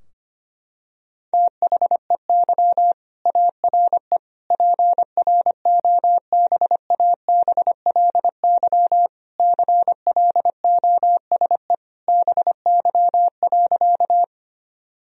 Morse code, heard as a keyed tone, about 25 wpm, 700 Hz. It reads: THEY ARE PROBABLY CLOSE BY.
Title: Sentences from Top 1000 Words - Review Part 4 - 25wpm